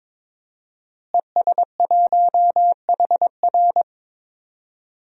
Morse code, heard as a keyed tone, about 22 wpm, 700 Hz. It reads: ES1HR
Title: DXCC-100 Call Signs - 5 Characters - Repeated 3x - 22wpm